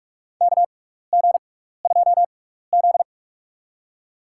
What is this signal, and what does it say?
Morse code, heard as a keyed tone, about 44 wpm, spaced as for 15 wpm, 700 Hz. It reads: KG2Z
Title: USA Call Signs - 4 Characters - 6x Character Spacing - 45wpm